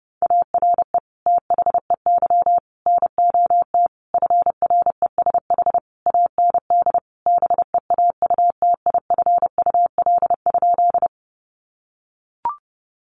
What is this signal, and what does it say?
Morse code, heard as a keyed tone, about 30 wpm, 700 Hz. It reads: ARE THEY NOT FRESH AND BEAUTIFUL?
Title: Sentences from Top 1200 Words 30wpm